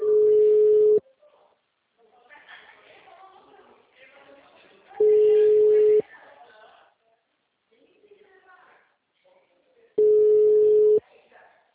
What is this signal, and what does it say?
Ringback tone (Europe, 425 Hz), with speech.
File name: out- -2009- - - .8545